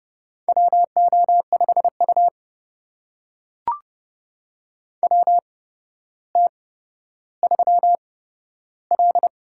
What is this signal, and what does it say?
Morse code, heard as a keyed tone, about 30 wpm, spaced as for 13 wpm, 700 Hz. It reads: WO5U WT3L
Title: USA Call Signs - 4 Characters - 8x Character Spacing - 30wpm